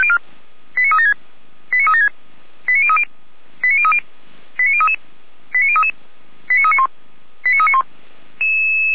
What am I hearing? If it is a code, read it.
Five-tone selective call (ZVEI), heard as quick runs of five tones, tones of about 70 ms. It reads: 89376 89376 89030 89030 89033 89033 89391 89391
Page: /455/SDR/ZVEI/89514/